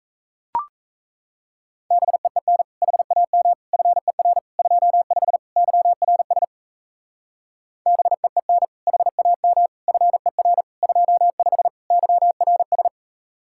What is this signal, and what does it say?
Morse code, heard as a keyed tone, about 34 wpm, 700 Hz. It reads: BEEN HAM FER 25 YRS BEEN HAM FER 25 YRS
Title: Speed-Racing Common QSO Phrases - 28wpm